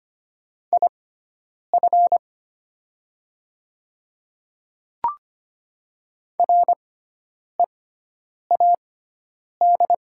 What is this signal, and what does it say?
Morse code, heard as a keyed tone, about 25 wpm, spaced as for 9 wpm, 700 Hz. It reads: IF READ
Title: 6x Character Spacing - Top 500 Words - 25wpm